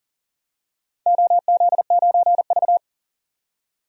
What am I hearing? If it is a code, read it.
Morse code, heard as a keyed tone, about 40 wpm, 700 Hz. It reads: OZ9V
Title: DXCC-100 Call Signs - 4 Characters - 40wpm